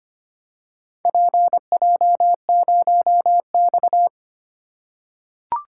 Morse code, heard as a keyed tone, about 25 wpm, 700 Hz. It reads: PJ0X
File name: practice / 25wpm / Four Letter-Number Combinations - 25wpm